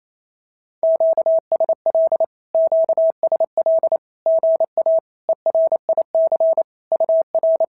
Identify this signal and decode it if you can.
Morse code, heard as a keyed tone, about 28 wpm, 650 Hz. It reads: QSL QSL GA ERIC UR
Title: SOTA QSO Practice 28wpm